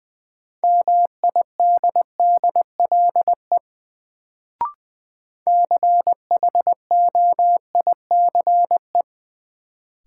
Morse code, heard as a keyed tone, about 20 wpm, 700 Hz. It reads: MIDDLE CHOICE
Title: Sentences from Top 900 Words - New Word Drill - 20wpm